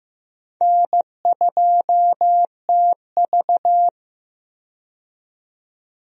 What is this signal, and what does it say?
Morse code, heard as a keyed tone, about 15 wpm, 700 Hz. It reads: N2TV
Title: USA Call Signs - 4 Characters - 15wpm